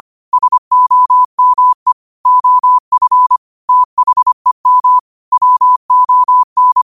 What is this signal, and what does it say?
Morse code, heard as a keyed tone, about 25 wpm, 1000 Hz. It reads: SOME OF THEM WON
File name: qotdfiles / 25wpm